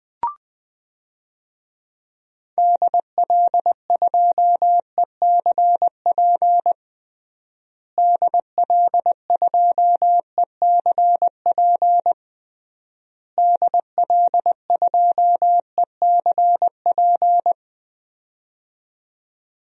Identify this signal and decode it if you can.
Morse code, heard as a keyed tone, about 19 wpm, 700 Hz. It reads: DL2ECP DL2ECP DL2ECP
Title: DXCC-100 Call Signs - 6 Characters - Repeated 3x - 20wpm